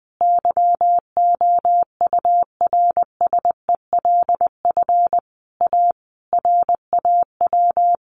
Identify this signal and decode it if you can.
Morse code, heard as a keyed tone, about 20 wpm, 700 Hz. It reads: YOURSELF A RAW
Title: Sentences from Top 1500 Words 20wpm